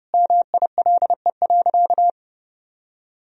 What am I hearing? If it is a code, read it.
Morse code, heard as a keyed tone, about 32 wpm, 700 Hz. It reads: MILE.
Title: Sentences from Top 1000 Words - Review Part 4 - 30wpm